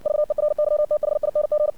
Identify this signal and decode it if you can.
Morse code, heard as a keyed tone, about 60 wpm, 600 Hz. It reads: VERYTHING
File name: archived / CRSnet